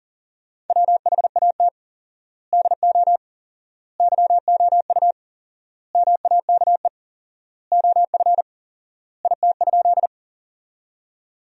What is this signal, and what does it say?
Morse code, heard as a keyed tone, about 40 wpm, 700 Hz. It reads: WHAT DO YOU MAKE OF IT?
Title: 4x Word-Spacing - Sentences from Top 100 Words - 40wpm